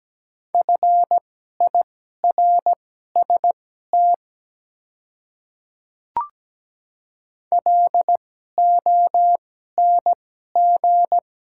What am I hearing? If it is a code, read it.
Morse code, heard as a keyed tone, about 17 wpm, spaced as for 12 wpm, 700 Hz. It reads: FIRST LONG
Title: 2x Character Spacing - Top 100 Words - 17wpm